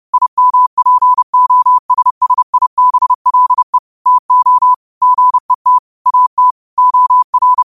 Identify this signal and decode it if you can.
Morse code, heard as a keyed tone, about 30 wpm, 1000 Hz. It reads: IMPOSSIBLE TO GET AT OR
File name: qotdfiles / 30wpm